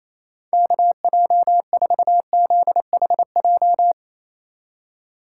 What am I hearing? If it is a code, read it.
Morse code, heard as a keyed tone, about 28 wpm, 700 Hz. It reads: KJ4ZHJ
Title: USA Call Signs - 6 Characters - 4x Character Spacing - 28wpm